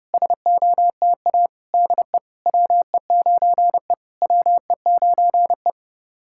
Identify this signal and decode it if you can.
Morse code, heard as a keyed tone, about 30 wpm, 700 Hz. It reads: SOTA DE WE9E WE9E
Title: SOTA QSO Practice 30wpm